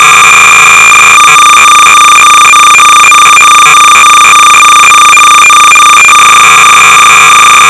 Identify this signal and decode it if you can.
Cassette data (Kansas City type).
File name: sorcerer / 300baud / Defender 3.2-(1982)(J.Forester-N.Leith)(System Software)-Sorcerer-300baud